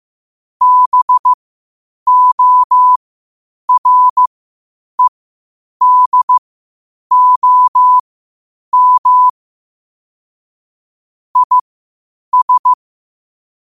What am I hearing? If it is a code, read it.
Morse code, heard as a keyed tone, about 15 wpm, spaced as for 8 wpm, 1000 Hz. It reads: BOREDOM IS
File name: qotdfiles / 8wpm